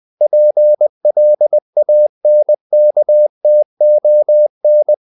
Morse code, heard as a keyed tone, about 20 wpm, 600 Hz. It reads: PLANKTON
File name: the old man and the sea / the old man and the sea part 1 20wpm